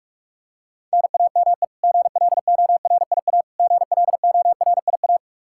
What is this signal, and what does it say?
Morse code, heard as a keyed tone, about 46 wpm, 700 Hz. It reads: NAME GLORIA GLORIA
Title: K1USN SST QSO Practice 45wpm